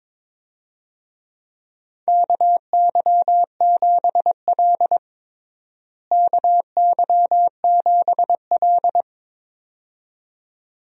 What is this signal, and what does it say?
Morse code, heard as a keyed tone, about 22 wpm, 700 Hz. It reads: KY7L KY7L
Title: USA Call Signs - 4 Characters - Repeated 2x - 22wpm